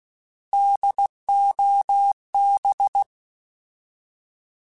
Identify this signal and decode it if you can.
Morse code, heard as a keyed tone, about 16 wpm, 800 Hz. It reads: DOB